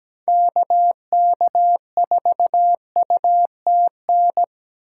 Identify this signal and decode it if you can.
Morse code, heard as a keyed tone, about 17 wpm, 700 Hz. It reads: KK4UTN
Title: USA Call Signs - 2x Character Spacing - 17wpm